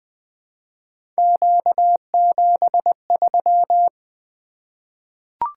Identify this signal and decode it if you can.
Morse code, heard as a keyed tone, about 20 wpm, 700 Hz. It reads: Q73
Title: Three Letter-Number Combinations - 20wpm